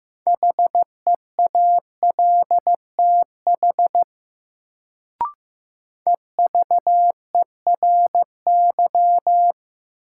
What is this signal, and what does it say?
Morse code, heard as a keyed tone, about 15 wpm, 700 Hz. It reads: HEALTH EVERY
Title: Sets of 5 Words - New Word Drill - 15wpm